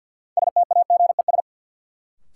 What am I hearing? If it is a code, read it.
Morse code, heard as a keyed tone, about 46 wpm, 700 Hz. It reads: STAGES
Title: Sets of 5 Words Encore - New Word Drill - 50wpm